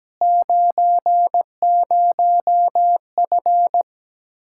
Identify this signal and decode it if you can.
Morse code, heard as a keyed tone, about 17 wpm, 700 Hz. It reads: 90F
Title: Practice QSO - Exchange 2 - Info - 17wpm